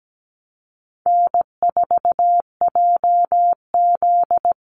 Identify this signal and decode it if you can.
Morse code, heard as a keyed tone, about 17 wpm, 700 Hz. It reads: N4JZ
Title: USA Call Signs 17wpm